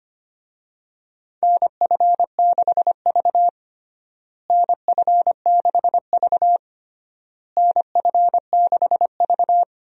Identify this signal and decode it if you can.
Morse code, heard as a keyed tone, about 25 wpm, 700 Hz. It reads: NF6V NF6V NF6V
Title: USA Call Signs - 4 Characters - Repeated 3x - 25wpm